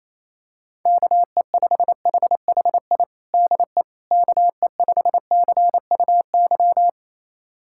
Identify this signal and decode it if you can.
Morse code, heard as a keyed tone, about 28 wpm, 700 Hz. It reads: KE5HHI DE KE5CUY
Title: USA Callsign DE Callsign - 6 Characters - 28wpm